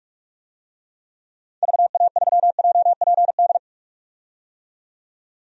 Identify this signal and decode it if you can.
Morse code, heard as a keyed tone, about 46 wpm, 700 Hz. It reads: VA3JPD
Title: DXCC-100 Call Signs - 6 Characters - 45wpm